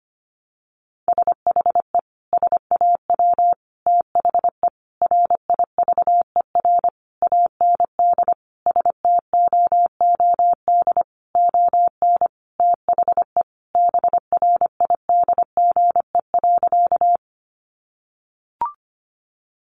Morse code, heard as a keyed tone, about 25 wpm, 700 Hz. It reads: SHE SAW THE RIVER AND STOOD ON THE BRIDGE.
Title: Sentences from Top 1200 Words 25wpm